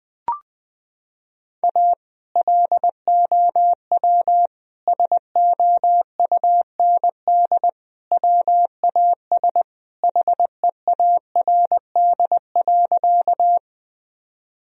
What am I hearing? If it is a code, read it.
Morse code, heard as a keyed tone, about 21 wpm, 700 Hz. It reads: A LOW SOUND WAS HEARD.
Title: Sentences from Top 1000 Words - Review Part 1 - 20wpm